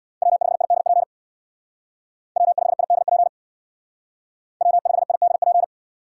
Morse code, heard as a keyed tone, about 61 wpm, 700 Hz. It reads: W5IDP W5IDP W5IDP
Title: Speed Racing - USA Call Signs 50wpm